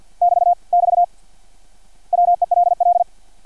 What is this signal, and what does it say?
Morse code, heard as a keyed tone, about 50 wpm, 700 Hz. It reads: <BT> <BT> WELL
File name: archived / CRSnet